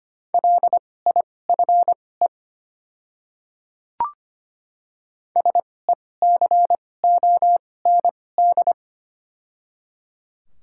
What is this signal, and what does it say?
Morse code, heard as a keyed tone, about 25 wpm, spaced as for 18 wpm, 700 Hz. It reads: LIFE SECOND
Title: 2x Character Spacing - Top 300 Words - 25wpm